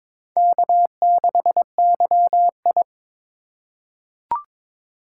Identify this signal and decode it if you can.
Morse code, heard as a keyed tone, about 22 wpm, 700 Hz. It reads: K6YI
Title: USA Call Signs - 4x Character Spacing - 22wpm